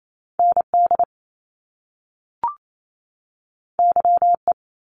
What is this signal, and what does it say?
Morse code, heard as a keyed tone, about 28 wpm, 700 Hz. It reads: ND YE
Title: Top Two Letter Combinations in Words 28wpm